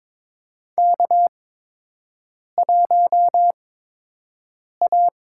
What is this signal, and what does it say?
Morse code, heard as a keyed tone, about 22 wpm, spaced as for 6 wpm, 700 Hz. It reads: K1A
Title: USA Call Signs - 8x Character Spacing - 22wpm